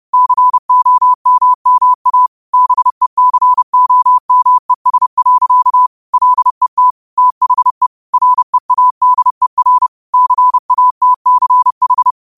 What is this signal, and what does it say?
Morse code, heard as a keyed tone, about 30 wpm, 1000 Hz. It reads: COMMA BECOMES. LET THE READER CATCH